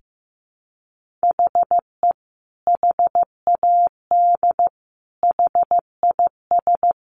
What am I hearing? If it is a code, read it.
Morse code, heard as a keyed tone, about 15 wpm, 700 Hz. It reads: HE HAD HIS
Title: Sentences from Top 1500 Words 15wpm